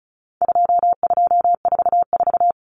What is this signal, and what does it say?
Morse code, heard as a keyed tone, about 34 wpm, 700 Hz. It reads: 2244
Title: Common Cut Numbers - 4-Digit 35wpm